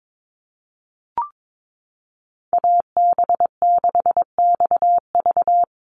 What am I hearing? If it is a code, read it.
Morse code, heard as a keyed tone, about 22 wpm, 700 Hz. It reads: AB6XV
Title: USA Call Signs Repeated 2x 22wpm